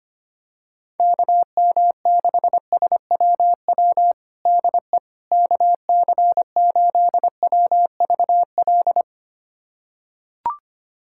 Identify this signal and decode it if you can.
Morse code, heard as a keyed tone, about 25 wpm, 700 Hz. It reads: KM6SWW DE KC8WVL
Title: USA Callsign DE Callsign 25wpm